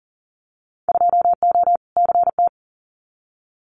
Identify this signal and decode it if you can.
Morse code, heard as a keyed tone, about 41 wpm, 700 Hz. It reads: 2O CT
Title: Winter Field Day Exchanges 40wpm